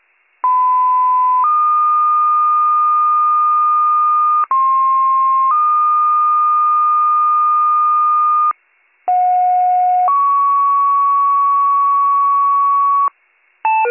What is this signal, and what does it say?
Two-tone page, heard as a one-second tone, then a three-second tone, about 1010 then 1220 Hz, about 1060 then 1220 Hz, about 730 then 1120 Hz. It is heard from 0.4 to 13.1 s.